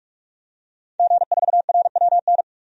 Morse code, heard as a keyed tone, about 46 wpm, 700 Hz. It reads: G4RWN
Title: DXCC-100 Call Signs - 5 Characters - 45wpm